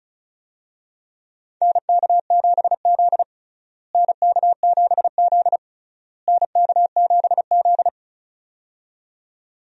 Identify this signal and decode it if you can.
Morse code, heard as a keyed tone, about 34 wpm, 700 Hz. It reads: NK7Z NK7Z NK7Z
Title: USA Call Signs - 4 Characters - Repeated 3x - 35wpm